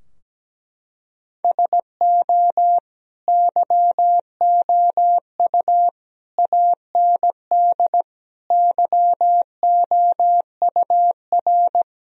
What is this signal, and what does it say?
Morse code, heard as a keyed tone, about 17 wpm, 700 Hz. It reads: SO YOU AND YOUR
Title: Sentences from Top 1000 Words - Review Part 4 - 17wpm